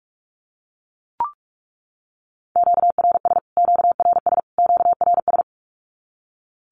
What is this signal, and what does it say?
Morse code, heard as a keyed tone, about 44 wpm, 700 Hz. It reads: QRS QRS QRS
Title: Q-Codes Repeated 3x 45wpm